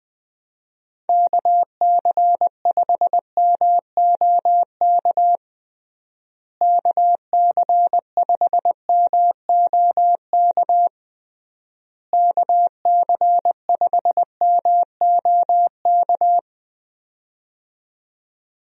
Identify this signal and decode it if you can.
Morse code, heard as a keyed tone, about 21 wpm, 700 Hz. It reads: KC5MOK KC5MOK KC5MOK
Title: USA Call Signs - 6 Characters - Repeated 3x - 20wpm